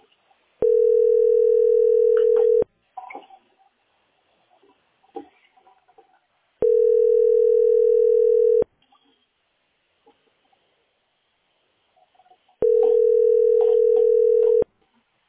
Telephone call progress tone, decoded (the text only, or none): ringback tone